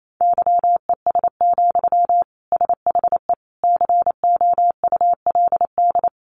Morse code, heard as a keyed tone, about 28 wpm, 700 Hz. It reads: YES, SHE COULD
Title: Sentences from Top 500 Words 28wpm